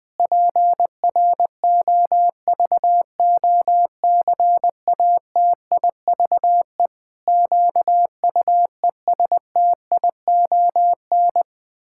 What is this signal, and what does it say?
Morse code, heard as a keyed tone, about 20 wpm, 700 Hz. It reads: PROVOCATIVE QUESTION
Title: Pangram Challenge v2 - 20wpm